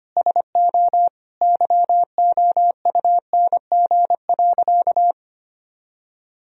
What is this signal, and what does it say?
Morse code, heard as a keyed tone, about 25 wpm, 700 Hz. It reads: SO YOUNG.
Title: Sentences from Top 400 Words and Abbreviations 25wpm